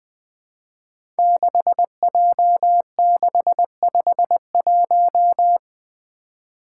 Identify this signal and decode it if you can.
Morse code, heard as a keyed tone, about 20 wpm, 700 Hz. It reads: 6J651
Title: Five Challenging Letter-Number Combinations 20wpm